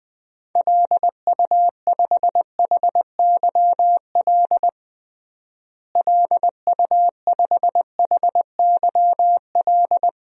Morse code, heard as a keyed tone, about 20 wpm, 700 Hz. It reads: LU5HYL LU5HYL
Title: DXCC-100 Call Signs - 6 Characters - Repeated 2x - 20wpm